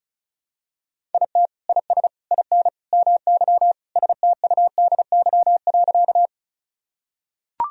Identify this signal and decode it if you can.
Morse code, heard as a keyed tone, about 36 wpm, 700 Hz. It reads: IT IS IN MY STUDY.
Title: Sentences from Top 1000 Words - Review Part 1 - 35wpm